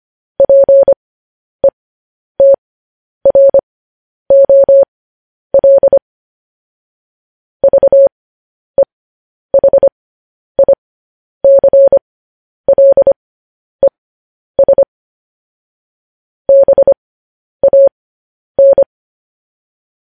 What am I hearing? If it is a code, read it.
Morse code, heard as a keyed tone, about 25 wpm, spaced as for 10 wpm, 550 Hz. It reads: PETROL VEHICLES BAN
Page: news headlines in morse code at 10 wpm